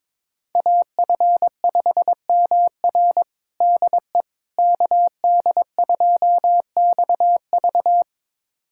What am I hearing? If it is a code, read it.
Morse code, heard as a keyed tone, about 22 wpm, 700 Hz. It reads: AF5MR DE KD2XV
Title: USA Callsign DE Callsign - 5 Characters - 22wpm